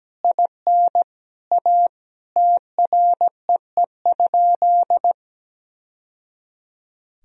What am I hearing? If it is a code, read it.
Morse code, heard as a keyed tone, about 17 wpm, 700 Hz. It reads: IN A TREE?
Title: Sentences from Top 1000 Words - Review Part 1 - 17wpm